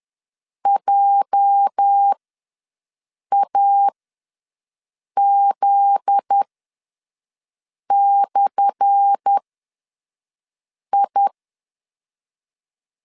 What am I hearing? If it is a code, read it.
Morse code, heard as a keyed tone, about 10 wpm, spaced as for 5 wpm, 800 Hz. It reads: JAZ/I
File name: LEZIONI / 10 SIGNS / 10A SLASH ws 10wpm 4121